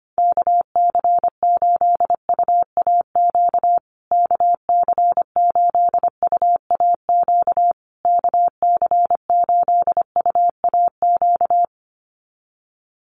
Morse code, heard as a keyed tone, about 25 wpm, 700 Hz. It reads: KC8UAQ KC8UAQ KC8UAQ
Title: USA Call Signs Repeated 3x 25wpm